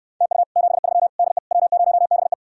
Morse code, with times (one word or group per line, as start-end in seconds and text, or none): 0.2-0.4 TU
0.6-1.1 73
1.2-1.4 DE
1.5-2.3 W0BE